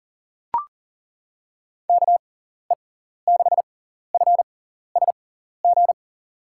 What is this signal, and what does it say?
Morse code, heard as a keyed tone, about 40 wpm, spaced as for 14 wpm, 700 Hz. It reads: KE6FSG
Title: USA Call Signs - 6 Characters - 6x Character Spacing - 40wpm